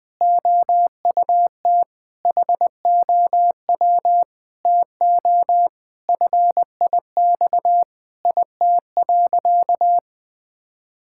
Morse code, heard as a keyed tone, about 21 wpm, 700 Hz. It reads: OUT HOW TO FIX IT.
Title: Sentences with 6-Word Phrases - 20wpm